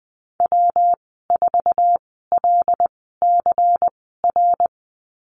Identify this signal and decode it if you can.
Morse code, heard as a keyed tone, about 20 wpm, spaced as for 15 wpm, 700 Hz. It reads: W4LCR
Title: 140 Lesson 23g - Letter f - Call Signs 20wpm